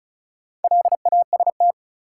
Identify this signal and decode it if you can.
Morse code, heard as a keyed tone, about 34 wpm, 700 Hz. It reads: LAST
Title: Sets of 2 Words - New Word Drill - 35wpm